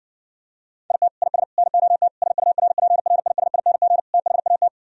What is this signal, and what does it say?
Morse code, heard as a keyed tone, about 59 wpm, 700 Hz. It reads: IT IS NOT SURPRISING THAT